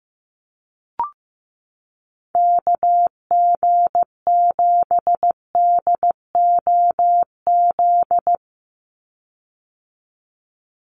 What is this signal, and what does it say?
Morse code, heard as a keyed tone, about 15 wpm, 700 Hz. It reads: KG7DOZ
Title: USA Call Signs 15wpm